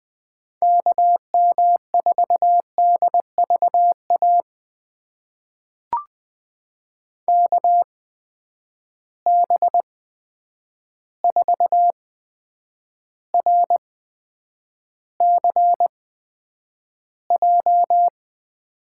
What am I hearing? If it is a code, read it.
Morse code, heard as a keyed tone, about 20 wpm, spaced as for 9 wpm, 700 Hz. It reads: KM4DVA KB4RCJ
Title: USA Call Signs - 6 Characters - 8x Character Spacing - 20wpm